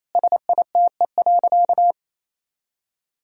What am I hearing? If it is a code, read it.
Morse code, heard as a keyed tone, about 28 wpm, 700 Hz. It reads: SITE.
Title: Sentences from Top 1000 Words - Review Part 4 - 28wpm